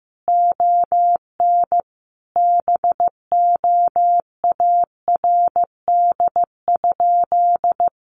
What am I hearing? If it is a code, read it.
Morse code, heard as a keyed tone, about 15 wpm, 700 Hz. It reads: ON BOARD?
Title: Sentences from Top 700 Words 15wpm